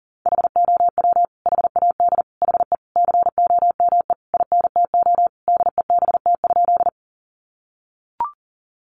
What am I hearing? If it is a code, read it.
Morse code, heard as a keyed tone, about 40 wpm, 700 Hz. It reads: HOW HAD HE COME INTO DEBT?
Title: Sentences from Top 1300 Words 40wpm